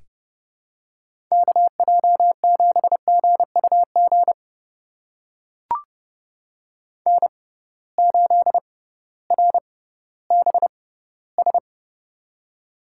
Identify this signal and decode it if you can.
Morse code, heard as a keyed tone, about 30 wpm, spaced as for 16 wpm, 700 Hz. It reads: KJ7GUG N8RBS
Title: USA Call Signs - 6x Character Spacing - 30wpm